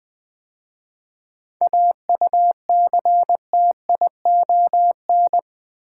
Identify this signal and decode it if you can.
Morse code, heard as a keyed tone, about 20 wpm, 700 Hz. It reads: AUCTION
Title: 7 Letter Words - First 3 Letters Given - 20wpm